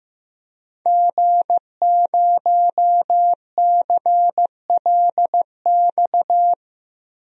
Morse code, heard as a keyed tone, about 15 wpm, 700 Hz. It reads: G0CLX
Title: DXCC-100 Call Signs - 5 Characters - 15wpm